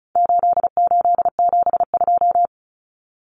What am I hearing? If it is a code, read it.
Morse code, heard as a keyed tone, about 34 wpm, 700 Hz. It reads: 8872
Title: Numbers - 4-Digit 35wpm